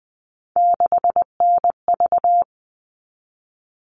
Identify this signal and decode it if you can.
Morse code, heard as a keyed tone, about 20 wpm, 700 Hz. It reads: 6NV
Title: All Cut Numbers - 3-Digit 20wpm